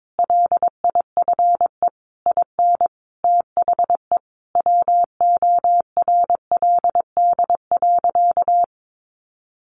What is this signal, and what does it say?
Morse code, heard as a keyed tone, about 22 wpm, 700 Hz. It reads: LIFE IN THE WORLD.
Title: Sentences from Top 1100 Words 22wpm